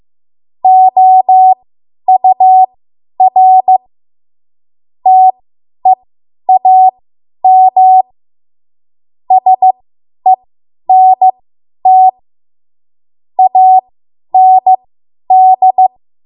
Morse code, heard as a keyed tone, about 15 wpm, spaced as for 10 wpm, 750 Hz. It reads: OUR TEAM SENT AND
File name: Morse / 10WPM